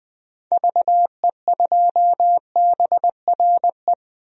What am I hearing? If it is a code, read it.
Morse code, heard as a keyed tone, about 20 wpm, 700 Hz. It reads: VE2BRE
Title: DXCC-100 Call Signs - 6 Characters - Repeated 3x - 20wpm